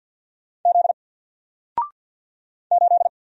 Morse code, heard as a keyed tone, about 50 wpm, 700 Hz. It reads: Z 8
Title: Single Letter-Number 50wpm